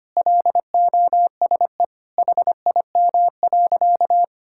Morse code, heard as a keyed tone, about 25 wpm, 700 Hz. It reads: LOSE HIM.
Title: Sentences from Top 1600 Words 25wpm